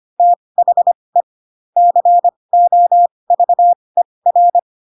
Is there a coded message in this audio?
Yes: Morse code from 0.2 to 4.6 s, heard as a keyed tone, about 25 wpm, spaced as for 20 wpm, 700 Hz. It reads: THE COVER